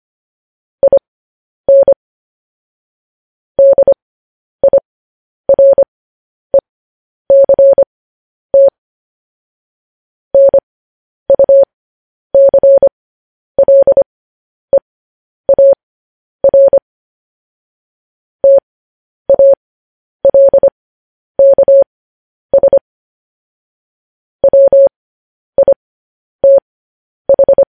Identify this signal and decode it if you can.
Morse code, heard as a keyed tone, about 25 wpm, spaced as for 10 wpm, 550 Hz. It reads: IN DIRECT NUCLEAR TALKS WITH
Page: news headlines in morse code at 10 wpm